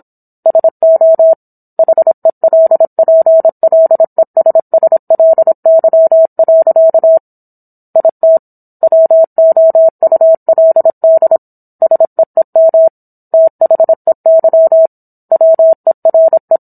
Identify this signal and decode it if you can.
Morse code, heard as a keyed tone, about 27 wpm, 650 Hz. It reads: SO HELPLESSLY. IT WOULD SEEM THEY WERE